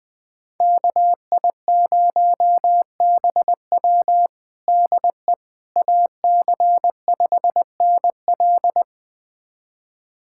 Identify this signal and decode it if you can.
Morse code, heard as a keyed tone, about 20 wpm, 700 Hz. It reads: KI0BW DE AC5NL
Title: USA Callsign DE Callsign - 5 Characters - 20wpm